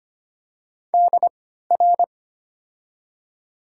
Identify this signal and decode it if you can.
Morse code, heard as a keyed tone, about 25 wpm, spaced as for 14 wpm, 700 Hz. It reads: DR
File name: Top 90 QSO Elements - 3x Character Spacing - 25wpm